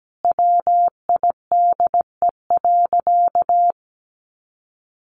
Morse code, heard as a keyed tone, about 17 wpm, 700 Hz. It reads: WIDE.
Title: Sentences from Top 1200 Words 17wpm